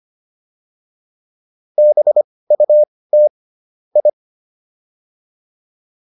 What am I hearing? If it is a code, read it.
Morse code, heard as a keyed tone, about 25 wpm, spaced as for 18 wpm, 600 Hz. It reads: BUT I